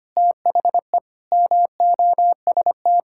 Morse code, heard as a keyed tone, about 25 wpm, 700 Hz. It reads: THE MOST
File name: practice / 25wpm / Sentences with 9-Word Phrases - 25wpm